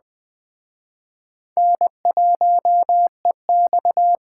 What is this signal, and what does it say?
Morse code, heard as a keyed tone, about 21 wpm, 700 Hz. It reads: N1EX
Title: Speed Racing - USA 1x2 Call Signs 17wpm